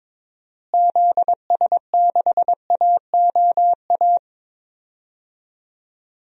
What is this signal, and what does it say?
Morse code, heard as a keyed tone, about 21 wpm, 700 Hz. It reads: ZS6AOA
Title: DXCC-100 Call Signs - 6 Characters - Repeated 3x - 22wpm